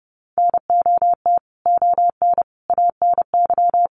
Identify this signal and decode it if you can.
Morse code, heard as a keyed tone, about 30 wpm, 700 Hz. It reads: NOT ON ANY
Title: Sentences from Top 900 Words 30wpm